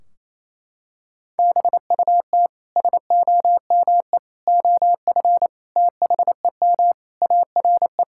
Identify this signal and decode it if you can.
Morse code, heard as a keyed tone, about 28 wpm, 700 Hz. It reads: BUT SOME OF THEM ARE
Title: Sentences from Top 1600 Words 28wpm